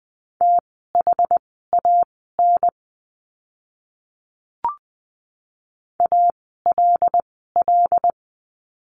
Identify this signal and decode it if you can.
Morse code, heard as a keyed tone, about 20 wpm, spaced as for 15 wpm, 700 Hz. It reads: THAN ALL
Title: 115 Lesson 19f - Letter w - Words 20wpm